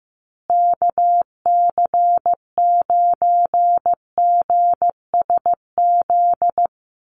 Morse code, heard as a keyed tone, about 14 wpm, 700 Hz. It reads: KC9GSZ
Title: USA Call Signs Repeated 3x 15wpm